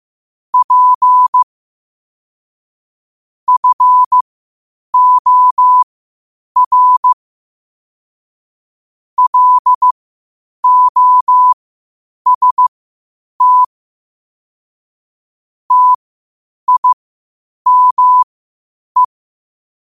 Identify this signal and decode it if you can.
Morse code, heard as a keyed tone, about 15 wpm, spaced as for 8 wpm, 1000 Hz. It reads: P FOR LOST TIME
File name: qotdfiles / 8wpm